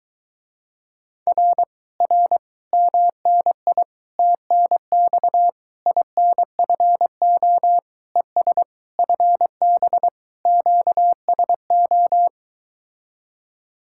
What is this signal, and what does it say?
Morse code, heard as a keyed tone, about 23 wpm, 700 Hz. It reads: R R MNI TNX INFO ES FB QSO